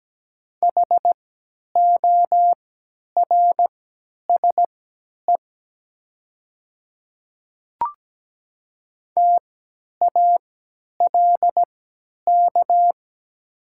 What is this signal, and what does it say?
Morse code, heard as a keyed tone, about 17 wpm, spaced as for 10 wpm, 700 Hz. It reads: HORSE TALK
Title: 3x Character Spacing - Top 500 Words - 17wpm